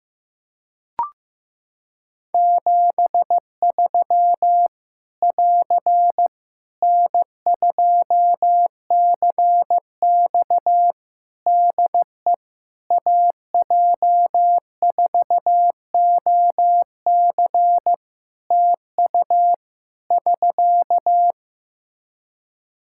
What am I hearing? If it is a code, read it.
Morse code, heard as a keyed tone, about 15 wpm, 700 Hz. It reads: 73 <AR> N2CX DE AJ4OC TU <SK>
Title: Practice QSO - Exchange 4 - Ending - 15wpm